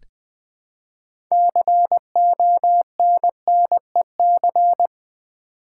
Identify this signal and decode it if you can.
Morse code, heard as a keyed tone, about 20 wpm, 700 Hz. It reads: CONNEC
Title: Build A Word - 3 Letters At A Time - 9-Letter Words - 20wpm